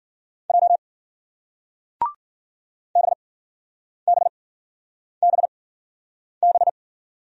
Morse code, heard as a keyed tone, about 51 wpm, spaced as for 9 wpm, 700 Hz. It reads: J BBBB